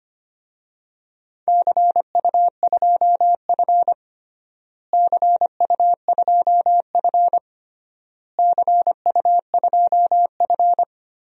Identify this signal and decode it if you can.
Morse code, heard as a keyed tone, about 25 wpm, 700 Hz. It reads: CU2F CU2F CU2F
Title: DXCC-100 Call Signs - 4 Characters - Repeated 3x - 25wpm